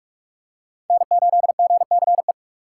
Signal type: Morse code

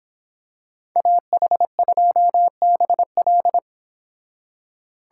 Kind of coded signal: Morse code

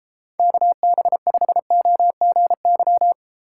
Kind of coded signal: Morse code

